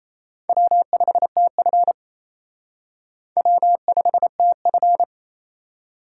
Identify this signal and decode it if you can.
Morse code, heard as a keyed tone, about 37 wpm, 700 Hz. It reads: W5TF W5TF